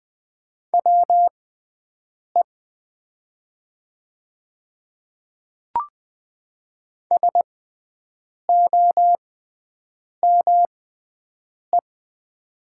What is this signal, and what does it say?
Morse code, heard as a keyed tone, about 20 wpm, spaced as for 7 wpm, 700 Hz. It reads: WE SOME
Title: 6x Character Spacing - Top 300 Words - 20wpm